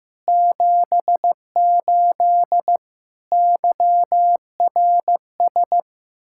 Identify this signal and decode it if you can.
Morse code, heard as a keyed tone, about 15 wpm, 700 Hz. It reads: 78 YRS
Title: Practice QSO - Exchange 3 - More Info - 15wpm